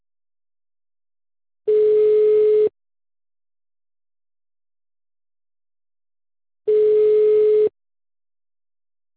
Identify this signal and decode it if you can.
Ringback tone (Europe, 425 Hz).